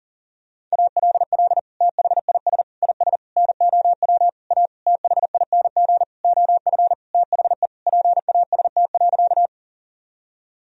Morse code, heard as a keyed tone, about 40 wpm, 700 Hz. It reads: ALL THIS IS NOW A THING OF THE PAST.
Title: Sentences from Top 1000 Words - Review Part 3 - 40wpm